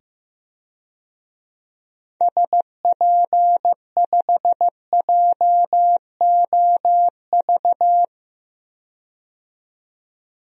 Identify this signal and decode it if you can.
Morse code, heard as a keyed tone, about 15 wpm, 700 Hz. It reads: SP5JOV
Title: DXCC-100 Call Signs - 6 Characters - 15wpm